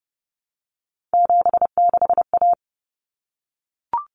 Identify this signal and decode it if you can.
Morse code, heard as a keyed tone, about 30 wpm, 700 Hz. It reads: 76A
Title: Common Cut Numbers - 3-Digit 30wpm